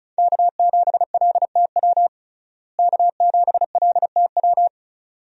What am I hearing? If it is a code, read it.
Morse code, heard as a keyed tone, about 34 wpm, 700 Hz. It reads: K7LTW K7LTW
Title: USA Call Signs - 5 Characters - Repeated 2x - 35wpm